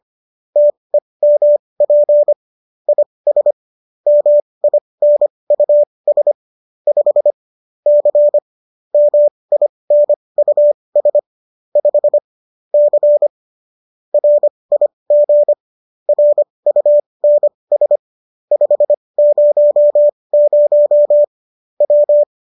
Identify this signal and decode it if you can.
Morse code, heard as a keyed tone, about 25 wpm, spaced as for 20 wpm, 590 Hz. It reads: TEMP IS MINUS 5 C MINUS 5 C RIG RUNS 500 W
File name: qso219 20